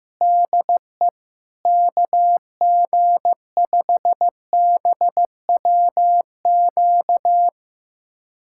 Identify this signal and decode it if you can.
Morse code, heard as a keyed tone, about 15 wpm, 700 Hz. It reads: DE KG5BWQ
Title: USA Callsign DE Callsign 15wpm